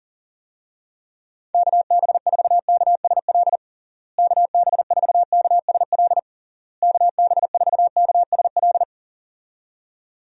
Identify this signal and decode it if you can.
Morse code, heard as a keyed tone, about 40 wpm, 700 Hz. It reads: KB4KSL KB4KSL KB4KSL
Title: USA Call Signs - 6 Characters - Repeated 3x - 40wpm